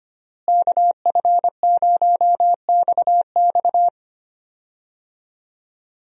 Morse code, heard as a keyed tone, about 25 wpm, 700 Hz. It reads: KF0XX
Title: USA Call Signs - 5 Characters - Repeated 2x - 25wpm